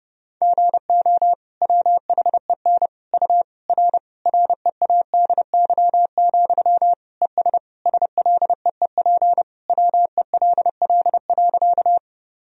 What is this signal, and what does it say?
Morse code, heard as a keyed tone, about 30 wpm, 700 Hz. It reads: GO WHEN U R READY, ES SLEEP WELL.
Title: Sentences from Top 500 Words and Abbreviations 30wpm